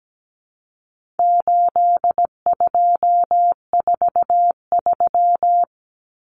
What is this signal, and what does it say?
Morse code, heard as a keyed tone, about 17 wpm, 700 Hz. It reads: 8243
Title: Common Cut Numbers - 4-Digit 17wpm